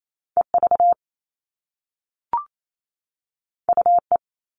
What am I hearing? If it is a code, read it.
Morse code, heard as a keyed tone, about 28 wpm, 700 Hz. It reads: EV UE
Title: All Cut Numbers - 2-Digit 28wpm